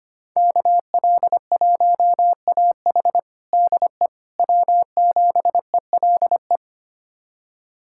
Morse code, heard as a keyed tone, about 25 wpm, 700 Hz. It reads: KL1AH DE W7ELE
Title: USA Callsign DE Callsign 25wpm